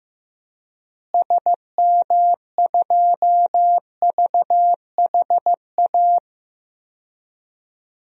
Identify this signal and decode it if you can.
Morse code, heard as a keyed tone, about 15 wpm, 700 Hz. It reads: SM2VHA
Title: DXCC-100 Call Signs - 6 Characters - Repeated 2x - 15wpm